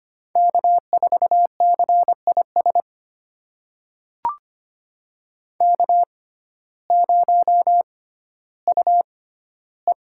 Morse code, heard as a keyed tone, about 25 wpm, spaced as for 14 wpm, 700 Hz. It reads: K4CIS K0UE
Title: USA Call Signs - 6x Character Spacing - 25wpm